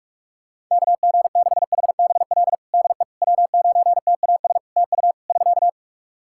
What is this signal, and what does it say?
Morse code, heard as a keyed tone, about 44 wpm, 700 Hz. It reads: KG6HBL DE W9TAS TU <SK>
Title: Practice QSO - Exchange 4 - Ending - 45wpm